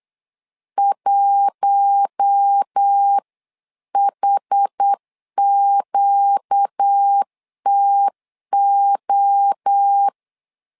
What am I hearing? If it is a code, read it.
Morse code, heard as a keyed tone, about 8 wpm, 800 Hz. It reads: 1HQTO
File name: LEZIONI / 4 Q1S 8wpm 2710